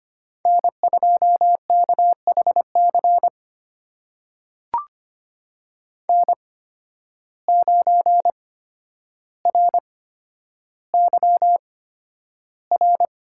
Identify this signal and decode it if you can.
Morse code, heard as a keyed tone, about 25 wpm, spaced as for 11 wpm, 700 Hz. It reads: N2KHC N9RYR
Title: USA Call Signs - 5 Characters - 8x Character Spacing - 25wpm